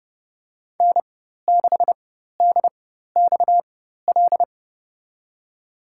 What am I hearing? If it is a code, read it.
Morse code, heard as a keyed tone, about 32 wpm, spaced as for 14 wpm, 700 Hz. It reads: N6DXL
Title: USA Call Signs - 5 Characters - 4x Character Spacing - 30wpm